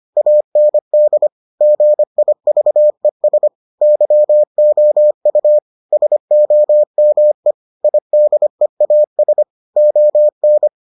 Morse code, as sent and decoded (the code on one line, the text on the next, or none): .- -. -.. / --. .. ...- . ... / -.-- --- ..- / ... --- -- . / .. -.. .- ... / --- -.
AND GIVES YOU SOME IDEAS ON